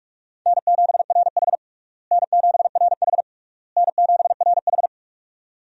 Morse code, heard as a keyed tone, about 46 wpm, 700 Hz. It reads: N7RH N7RH N7RH
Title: USA Call Signs - 4 Characters - Repeated 3x - 45wpm